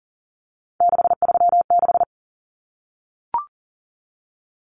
Morse code, heard as a keyed tone, about 40 wpm, 700 Hz. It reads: 636
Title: Common Cut Numbers - 3-Digit 40wpm